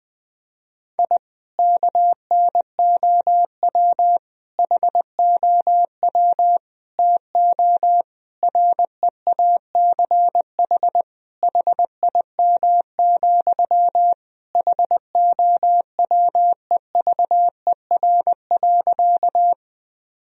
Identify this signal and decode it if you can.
Morse code, heard as a keyed tone, about 20 wpm, 700 Hz. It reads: I KNOW HOW TO REACH HIM, HOWEVER.
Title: Sentences from Top 1000 Words - Review Part 2 - 20wpm